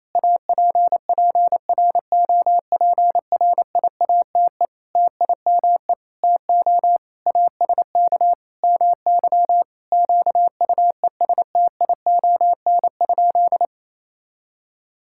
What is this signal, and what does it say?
Morse code, heard as a keyed tone, about 28 wpm, 700 Hz. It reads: APPROPRIATE TIME TO ASK MY QUESTION?
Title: Sentences from Top 1000 Words - Review Part 5 - 28wpm